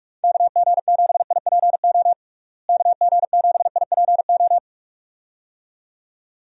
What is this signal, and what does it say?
Morse code, heard as a keyed tone, about 46 wpm, 700 Hz. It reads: KG7IPO KG7IPO
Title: USA Call Signs - 6 Characters - Repeated 2x - 45wpm